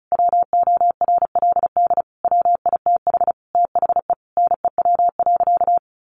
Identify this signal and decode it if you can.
Morse code, heard as a keyed tone, about 34 wpm, 700 Hz. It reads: WORLD WITH THE NEW.
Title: Sentences from Top 1300 Words 35wpm